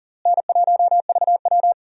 Morse code, heard as a keyed tone, about 40 wpm, 700 Hz. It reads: N1VW
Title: USA Call Signs - 4 Characters - Repeated 2x - 40wpm